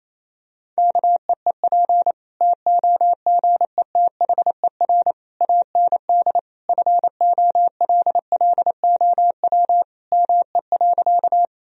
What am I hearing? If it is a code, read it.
Morse code, heard as a keyed tone, about 28 wpm, 700 Hz. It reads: KEEP TOGETHER AND FOLLOW ME.
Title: Sentences from Top 1000 Words - Review Part 1 - 28wpm